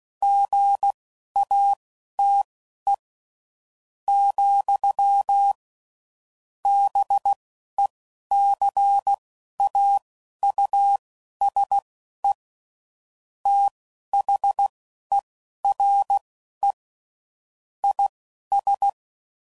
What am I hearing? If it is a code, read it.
Morse code, heard as a keyed tone, about 16 wpm, spaced as for 11 wpm, 800 Hz. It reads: GATE, BECAUSE THERE IS
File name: CW Code Groups @10WPM/Technical Subjects